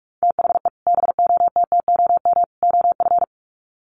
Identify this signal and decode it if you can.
Morse code, heard as a keyed tone, about 44 wpm, 700 Hz. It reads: THE BOTTOM OF
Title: Sets of 3 Words 45wpm